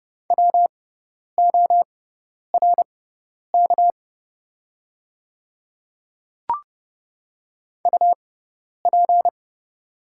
Morse code, heard as a keyed tone, about 30 wpm, spaced as for 10 wpm, 690 Hz. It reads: WORK UP
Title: 6x Character Spacing - Top 200 Words - 30wpm